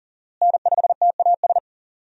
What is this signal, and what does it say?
Morse code, heard as a keyed tone, about 38 wpm, 700 Hz. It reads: N5TAS
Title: USA Call Signs - 5 Characters - Repeated 3x - 40wpm